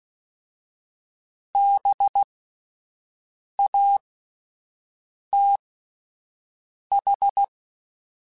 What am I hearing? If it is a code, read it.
Morse code, heard as a keyed tone, about 16 wpm, spaced as for 5 wpm, 800 Hz. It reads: BATH
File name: General / General-mp3s / Houses @10